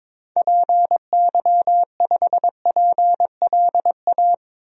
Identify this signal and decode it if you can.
Morse code, heard as a keyed tone, about 22 wpm, 700 Hz. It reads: PY5PLA
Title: DXCC-100 Call Signs - 6 Characters - 22wpm